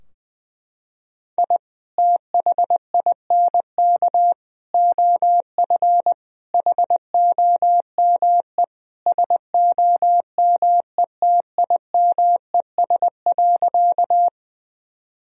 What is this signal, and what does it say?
Morse code, heard as a keyed tone, about 19 wpm, 700 Hz. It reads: I THINK OF HOME SOMETIMES.
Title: Sentences from Top 1000 Words - Review Part 3 - 20wpm